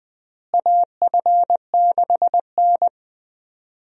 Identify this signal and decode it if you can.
Morse code, heard as a keyed tone, about 20 wpm, 700 Hz. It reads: AF6N
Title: USA Call Signs - 4 Characters - Repeated 2x - 20wpm